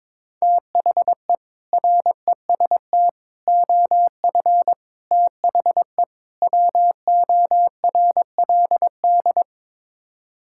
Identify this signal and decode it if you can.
Morse code, heard as a keyed tone, about 21 wpm, 700 Hz. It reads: THE REST OF THE WORLD
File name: practice / 22wpm / Build A Phrase - 5-Word Phrases - 22wpm